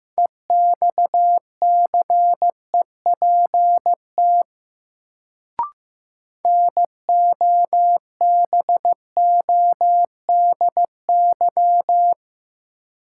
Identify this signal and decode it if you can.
Morse code, heard as a keyed tone, about 15 wpm, 700 Hz. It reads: EXCEPT NOBODY
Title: Sentences from Top 1600 Words - New Word Drill Alt - 15wpm